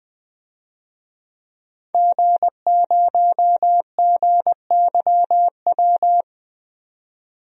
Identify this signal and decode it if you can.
Morse code, heard as a keyed tone, about 20 wpm, 700 Hz. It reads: G0GYW
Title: DXCC-100 Call Signs - 5 Characters - 20wpm